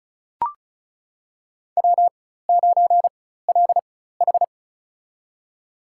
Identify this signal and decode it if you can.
Morse code, heard as a keyed tone, about 36 wpm, spaced as for 16 wpm, 700 Hz. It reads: W9LH
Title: USA Call Signs - 4x Character Spacing - 35wpm